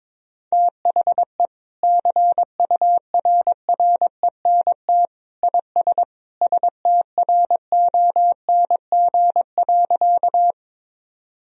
Morse code, heard as a keyed tone, about 21 wpm, 700 Hz. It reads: THE CURRENT IS STRONG.
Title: Sentences from Top 1000 Words - Review Part 4 - 22wpm